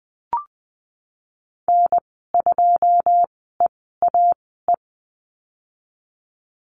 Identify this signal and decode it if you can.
Morse code, heard as a keyed tone, about 20 wpm, spaced as for 15 wpm, 700 Hz. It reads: N2EAE
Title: 128 Lesson 21g - Number 6 - Call Signs 20wpm